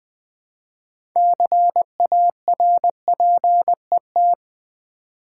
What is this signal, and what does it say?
Morse code, heard as a keyed tone, about 20 wpm, 700 Hz. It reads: CARPET
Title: 6 Letter Words - First 3 Letters Given - 20wpm